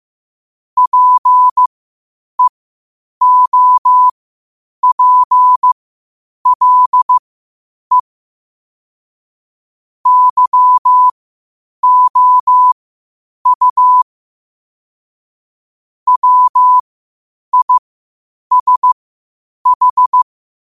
Morse code, heard as a keyed tone, about 15 wpm, spaced as for 8 wpm, 1000 Hz. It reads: PEOPLE YOU WISH
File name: qotdfiles / 8wpm